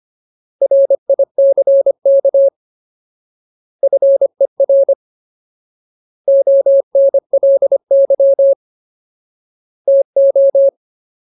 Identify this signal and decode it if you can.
Morse code, heard as a keyed tone, about 25 wpm, 550 Hz. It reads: RICK FER ONLY TO
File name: Ultimate Word List / 25WPM Ultimate